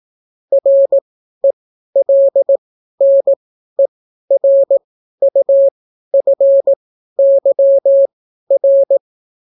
Morse code, heard as a keyed tone, about 18 wpm, spaced as for 12 wpm, 550 Hz. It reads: RELNERUFYR